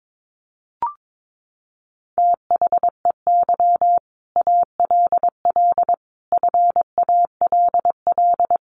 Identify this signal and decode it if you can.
Morse code, heard as a keyed tone, about 22 wpm, 700 Hz. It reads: THEY ALL FALL
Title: Sentences from Top 500 Words 22wpm